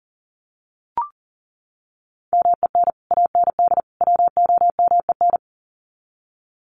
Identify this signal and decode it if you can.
Morse code, heard as a keyed tone, about 40 wpm, 700 Hz. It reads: MEN AND WOMEN
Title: Sets of 3 Words 40wpm